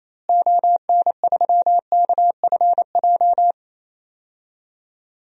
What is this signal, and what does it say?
Morse code, heard as a keyed tone, about 28 wpm, 700 Hz. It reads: ON3KFJ
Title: DXCC-100 Call Signs - 6 Characters - Repeated 3x - 28wpm